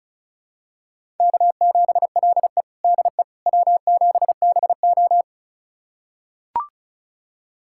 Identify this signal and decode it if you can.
Morse code, heard as a keyed tone, about 36 wpm, 700 Hz. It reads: K7LE DE W7BO
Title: USA Callsign DE Callsign - 4 Characters - 35wpm